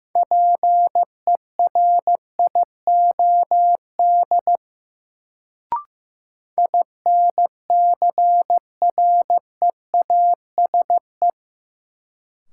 Morse code, heard as a keyed tone, about 15 wpm, 700 Hz. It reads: PERIOD INCREASE
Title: Sets of 5 Words - New Word Drill - 15wpm